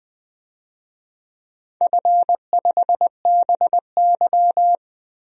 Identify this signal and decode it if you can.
Morse code, heard as a keyed tone, about 21 wpm, 700 Hz. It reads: F5BY
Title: DXCC-100 Call Signs - 4 Characters - Repeated 3x - 20wpm